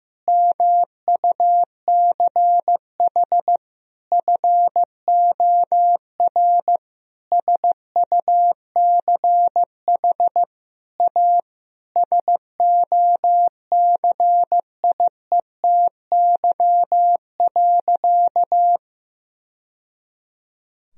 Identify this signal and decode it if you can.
Morse code, heard as a keyed tone, about 15 wpm, 700 Hz. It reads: MUCH FOR SUCH A SOCIETY.
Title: Sentences from Top 1000 Words - Review Part 4 - 15wpm